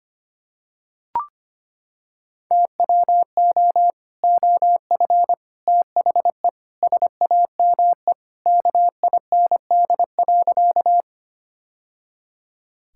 Morse code, heard as a keyed tone, about 25 wpm, 700 Hz. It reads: TWO OF THE SAME KIND.
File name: practice / 25wpm / Sentences with 3-Word Phrases - 25wpm